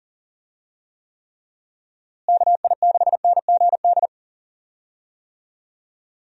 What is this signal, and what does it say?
Morse code, heard as a keyed tone, about 40 wpm, 700 Hz. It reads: KI6NGD
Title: USA Call Signs - 6 Characters - Repeated 3x - 40wpm